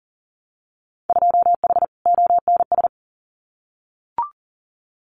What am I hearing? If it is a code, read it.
Morse code, heard as a keyed tone, about 40 wpm, 700 Hz. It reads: 2H ONS
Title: Winter Field Day Exchanges Encore 40wpm